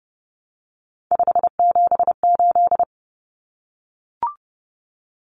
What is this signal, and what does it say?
Morse code, heard as a keyed tone, about 30 wpm, 700 Hz. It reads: 578
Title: Numbers - 3-Digit 30wpm